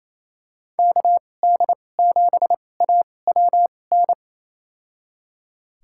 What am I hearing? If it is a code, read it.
Morse code, heard as a keyed tone, about 28 wpm, spaced as for 20 wpm, 700 Hz. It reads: KD7AWN
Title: USA Call Signs - 6 Characters - 2x Character Spacing - 28wpm